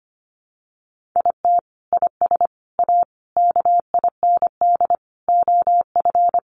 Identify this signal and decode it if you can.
Morse code, heard as a keyed tone, about 25 wpm, 700 Hz. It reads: IT IS A KIND OF
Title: Sentences from Top 1500 Words 25wpm